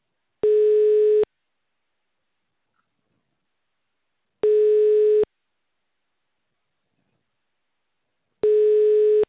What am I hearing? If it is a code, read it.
Ringback tone (425 Hz, 1 s on, 3 s off).